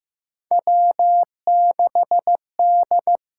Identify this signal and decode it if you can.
Morse code, heard as a keyed tone, about 15 wpm, 700 Hz. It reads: W6D
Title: USA Call Signs - 5 Characters - 6x Character Spacing - 15wpm